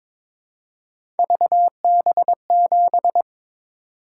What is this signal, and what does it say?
Morse code, heard as a keyed tone, about 22 wpm, 700 Hz. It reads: VB7
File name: practice / 22wpm / Three Letter-Number Combinations - 22wpm